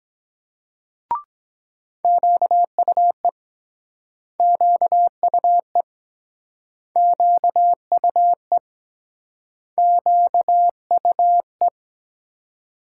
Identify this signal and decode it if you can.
Morse code, heard as a keyed tone, about 24 wpm, 700 Hz. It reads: QUE QUE QUE QUE